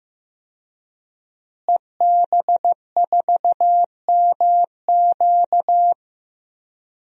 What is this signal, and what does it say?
Morse code, heard as a keyed tone, about 15 wpm, 700 Hz. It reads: EB4MQ
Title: DXCC-100 Call Signs - 5 Characters - Repeated 3x - 15wpm